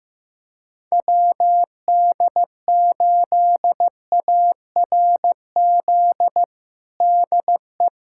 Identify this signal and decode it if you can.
Morse code, heard as a keyed tone, about 15 wpm, 700 Hz. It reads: WD8ARZ DE
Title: USA Callsign DE Callsign - 6 Characters - 15wpm